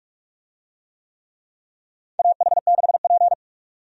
Morse code, heard as a keyed tone, about 43 wpm, 700 Hz. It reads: AH6P
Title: USA Call Signs - 4 Characters - 45wpm